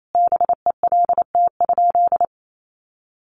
Morse code, heard as a keyed tone, about 28 wpm, 700 Hz. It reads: BELT?
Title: Sentences from Top 1400 Words 28wpm